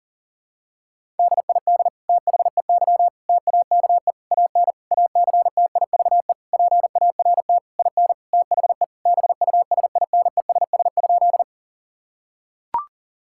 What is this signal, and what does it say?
Morse code, heard as a keyed tone, about 37 wpm, 700 Hz. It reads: DID THEY TAKE AN ACTIVE PART IN THE BUSINESS?
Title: Sentences from Top 1000 Words - Review Part 6 - 40wpm